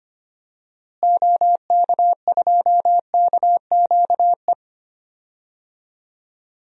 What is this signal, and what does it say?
Morse code, heard as a keyed tone, about 25 wpm, 700 Hz. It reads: OK2KQE
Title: DXCC-100 Call Signs - 6 Characters - Repeated 2x - 25wpm